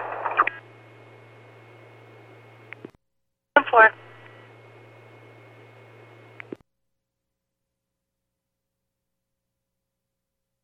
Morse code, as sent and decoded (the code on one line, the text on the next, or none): none